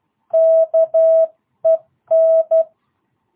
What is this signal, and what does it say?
Morse code, heard as a keyed tone, about 12 wpm, 650 Hz. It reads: KEN